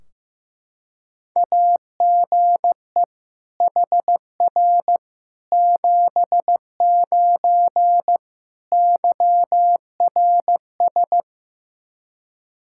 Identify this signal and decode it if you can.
Morse code, heard as a keyed tone, about 15 wpm, 700 Hz. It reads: AGE HR 79 YRS
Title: Practice QSO - Exchange 3 - More Info - 15wpm